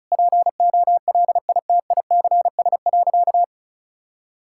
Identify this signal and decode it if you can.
Morse code, heard as a keyed tone, about 36 wpm, 700 Hz. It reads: POLITICS.
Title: Sentences from Top 1000 Words - Review Part 6 - 35wpm